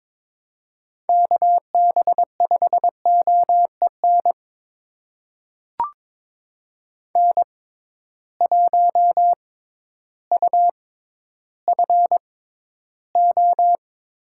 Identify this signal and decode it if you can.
Morse code, heard as a keyed tone, about 22 wpm, spaced as for 12 wpm, 700 Hz. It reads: KB5OEN N1UFO
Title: USA Call Signs - 6x Character Spacing - 22wpm